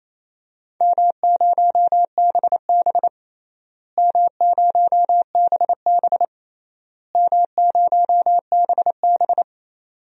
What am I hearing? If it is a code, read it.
Morse code, heard as a keyed tone, about 28 wpm, 700 Hz. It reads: M0BB M0BB M0BB
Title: DXCC-100 Call Signs - 4 Characters - Repeated 3x - 28wpm